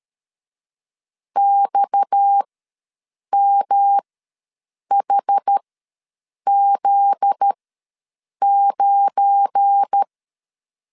Morse code, heard as a keyed tone, about 13 wpm, spaced as for 7 wpm, 800 Hz. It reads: XMHZ9
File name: LEZIONI / 9 2 ws 12wpm 3406